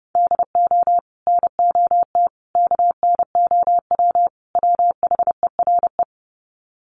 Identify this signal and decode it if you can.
Morse code, heard as a keyed tone, about 30 wpm, 700 Hz. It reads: DO NOT KNOW WHERE
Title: Sets of 4 Words 30wpm